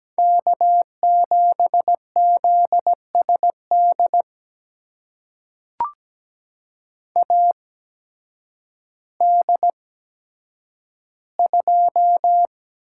Morse code, heard as a keyed tone, about 17 wpm, spaced as for 9 wpm, 700 Hz. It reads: K7ZSD AD2
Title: USA Call Signs - 8x Character Spacing - 17wpm